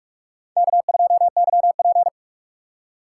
Morse code, heard as a keyed tone, about 46 wpm, 700 Hz. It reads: K2YP
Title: USA Call Signs - 4 Characters - 4x Character Spacing - 45wpm